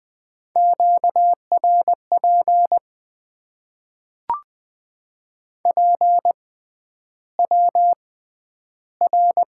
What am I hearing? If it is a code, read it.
Morse code, heard as a keyed tone, about 20 wpm, spaced as for 10 wpm, 700 Hz. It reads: QRP PWR
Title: Top 90 QSO Elements - 6x Character Spacing - 20wpm